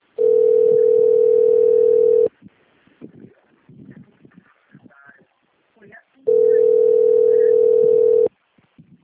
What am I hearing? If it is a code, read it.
Ringback tone (North America), 440 + 480 Hz, with speech.
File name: rg-1600-unknown- - - .28775